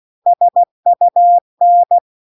Morse code, heard as a keyed tone, about 16 wpm, 700 Hz. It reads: SUN
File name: 16WPM